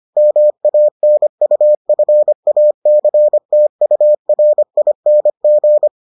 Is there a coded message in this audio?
Yes: Morse code from 0.2 to 5.9 s, heard as a keyed tone, about 25 wpm, 600 Hz. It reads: MANUFACTURING